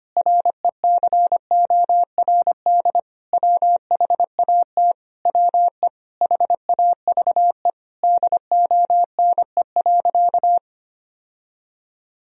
Morse code, heard as a keyed tone, about 25 wpm, 700 Hz. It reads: RECORD WHAT WE HAVE DONE.
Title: Sentences from Top 1000 Words - Review Part 3 - 25wpm